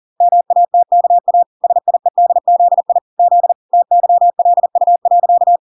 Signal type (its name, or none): Morse code